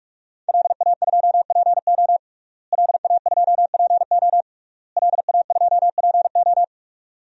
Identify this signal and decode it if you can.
Morse code, heard as a keyed tone, about 46 wpm, 700 Hz. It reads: LA2PO LA2PO LA2PO
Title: DXCC-100 Call Signs - 5 Characters - Repeated 3x - 45wpm